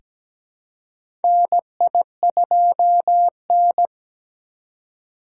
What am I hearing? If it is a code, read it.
Morse code, heard as a keyed tone, about 17 wpm, 700 Hz. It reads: NI2N
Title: USA Call Signs - 2x Character Spacing - 17wpm